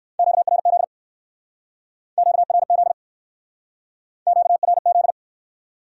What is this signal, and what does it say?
Morse code, heard as a keyed tone, about 59 wpm, 700 Hz. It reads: QRZ QRZ QRZ